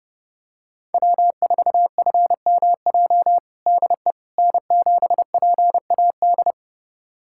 Morse code, heard as a keyed tone, about 30 wpm, 700 Hz. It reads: W4FMJ DE N7PAD
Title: USA Callsign DE Callsign - 5 Characters - 30wpm